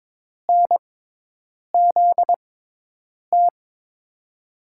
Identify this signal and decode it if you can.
Morse code, heard as a keyed tone, about 21 wpm, spaced as for 8 wpm, 700 Hz. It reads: NZT
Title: USA Call Signs - 6 Characters - 6x Character Spacing - 22wpm